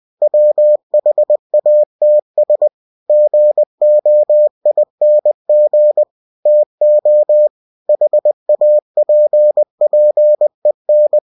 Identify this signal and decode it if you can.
Morse code, heard as a keyed tone, about 20 wpm, 600 Hz. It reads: WHATS GOING TO HAPPEN
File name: cw-books / winnie-the-pooh / winnie-20wpm-page-0110